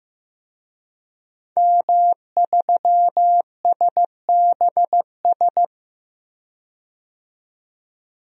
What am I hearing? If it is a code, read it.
Morse code, heard as a keyed tone, about 15 wpm, 700 Hz. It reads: M3SBS
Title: DXCC-100 Call Signs - 5 Characters - 15wpm